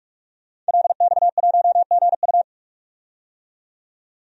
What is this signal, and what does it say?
Morse code, heard as a keyed tone, about 47 wpm, 700 Hz. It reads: LX1GU
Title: DXCC-100 Call Signs - 5 Characters - 45wpm